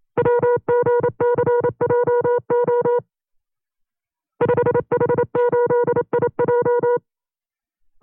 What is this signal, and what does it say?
Morse code, heard as a keyed tone, about 27 wpm, 480 Hz. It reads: WGCJO 5H8IJ
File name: CW-practice/25WPM